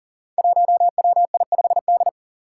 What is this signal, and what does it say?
Morse code, heard as a keyed tone, about 40 wpm, 700 Hz. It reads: 1WI5D
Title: Five Challenging Letter-Number Combinations 40wpm